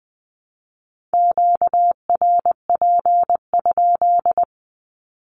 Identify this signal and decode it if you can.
Morse code, heard as a keyed tone, about 20 wpm, 700 Hz. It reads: QRP?
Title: Q-Codes 20wpm